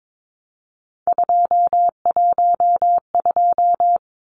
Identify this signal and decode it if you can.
Morse code, heard as a keyed tone, about 22 wpm, 700 Hz. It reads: 212
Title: Numbers - 3-Digit 22wpm